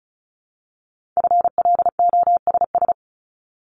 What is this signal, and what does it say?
Morse code, heard as a keyed tone, about 36 wpm, 700 Hz. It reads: FLOSS
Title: Multi-letter Phonemes Repeated 3x 35wpm